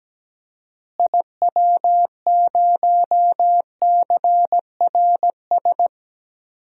Morse code, heard as a keyed tone, about 17 wpm, 700 Hz. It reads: IW0CRS
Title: DXCC-100 Call Signs - 6 Characters - Repeated 3x - 17wpm